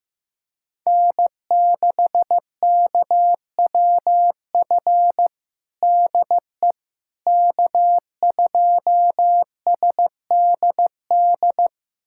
Morse code, heard as a keyed tone, about 15 wpm, 700 Hz. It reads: N6KWF DE K2SDD
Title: USA Callsign DE Callsign - 5 Characters - 15wpm